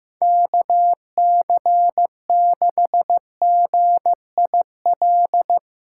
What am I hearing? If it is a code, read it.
Morse code, heard as a keyed tone, about 15 wpm, 700 Hz. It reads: KC6GIL
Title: USA Callsign DE Callsign - 6 Characters - 15wpm